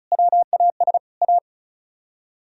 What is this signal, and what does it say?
Morse code, heard as a keyed tone, about 46 wpm, spaced as for 41 wpm, 700 Hz. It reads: WAS A
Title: Speed-Racing - Sets of 4 Words 35wpm